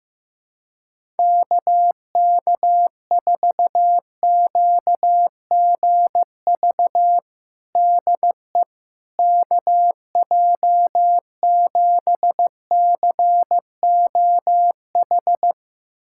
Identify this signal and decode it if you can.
Morse code, heard as a keyed tone, about 16 wpm, 700 Hz. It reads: KK4QGV DE KJ7COH
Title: USA Callsign DE Callsign 15wpm